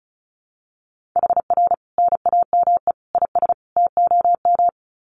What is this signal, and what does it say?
Morse code, heard as a keyed tone, about 34 wpm, 700 Hz. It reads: HR NAME IS TOM
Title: Common QSO Phrases 35wpm